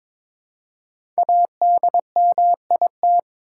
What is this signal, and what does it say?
Morse code, heard as a keyed tone, about 22 wpm, 700 Hz. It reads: ADMIT
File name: 5 Letter Words - First 3 Letters Given - 22wpm